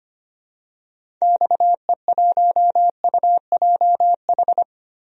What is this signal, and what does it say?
Morse code, heard as a keyed tone, about 25 wpm, 700 Hz. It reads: XE1UJH
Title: DXCC-100 Call Signs - 6 Characters - Repeated 3x - 25wpm